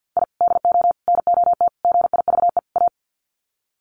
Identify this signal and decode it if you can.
Morse code, heard as a keyed tone, about 50 wpm, 700 Hz. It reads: I DO NOT GIVE A
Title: Sets of 5 Words Encore2 50wpm